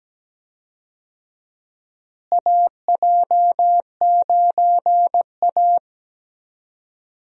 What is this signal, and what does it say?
Morse code, heard as a keyed tone, about 17 wpm, 700 Hz. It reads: AJ9A
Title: USA Call Signs - 4 Characters - Repeated 3x - 17wpm